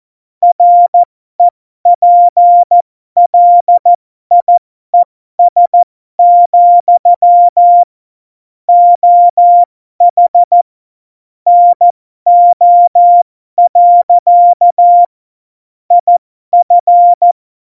Morse code, heard as a keyed tone, about 14 wpm, spaced as for 12 wpm, 700 Hz. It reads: REPLIES, OH NO. IF
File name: Morsekurs / Witze